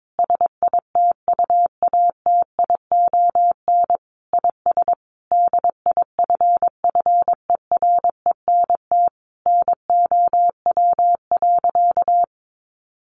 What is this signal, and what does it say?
Morse code, heard as a keyed tone, about 22 wpm, 700 Hz. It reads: SITUATION IS DIFFERENT NOW.
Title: Sentences from Top 1200 Words 22wpm